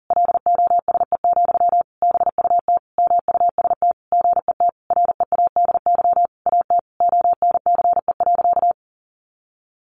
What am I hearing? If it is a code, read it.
Morse code, heard as a keyed tone, about 40 wpm, 700 Hz. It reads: LOSE, BUT MUST GET READY AT ONCE.
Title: Sentences from Top 1100 Words 40wpm